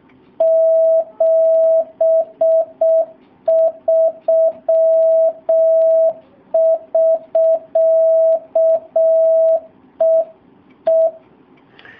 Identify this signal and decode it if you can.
Morse code, heard as a keyed tone, about 6 wpm, 650 Hz. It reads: 73 <SK> E E
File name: jh8cbh / 2019cw / SP07- w